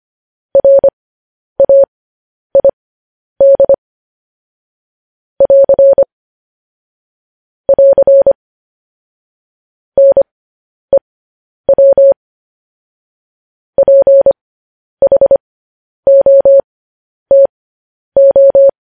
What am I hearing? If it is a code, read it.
Morse code, heard as a keyed tone, about 25 wpm, spaced as for 10 wpm, 550 Hz. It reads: RAID <AR> <AR> NEW PHOTO